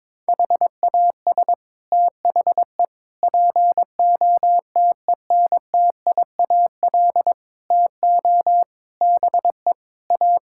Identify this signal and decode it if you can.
Morse code, heard as a keyed tone, about 21 wpm, 700 Hz. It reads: HAS THE POTENTIAL TO BE A
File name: practice / 22wpm / Sets of 6 Words - 22wpm